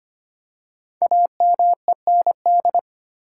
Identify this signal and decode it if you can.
Morse code, heard as a keyed tone, about 25 wpm, 700 Hz. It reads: AMEND
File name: 5 Letter Words - First 3 Letters Given - 25wpm